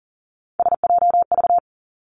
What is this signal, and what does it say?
Morse code, heard as a keyed tone, about 40 wpm, 700 Hz. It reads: SJV
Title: ARRL Sections 40wpm